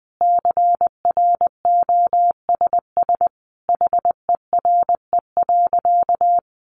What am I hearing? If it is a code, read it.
Morse code, heard as a keyed tone, about 20 wpm, 700 Hz. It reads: CROSS HERE.
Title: Sentences from Top 400 Words 20wpm